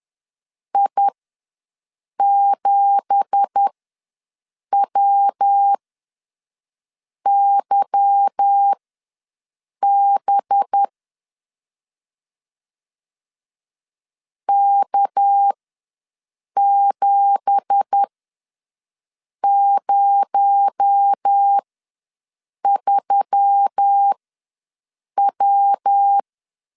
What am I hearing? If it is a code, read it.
Morse code, heard as a keyed tone, about 10 wpm, spaced as for 5 wpm, 800 Hz. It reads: I7WYB K703W
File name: LEZIONI / 9 2867 ws 10wpm 4215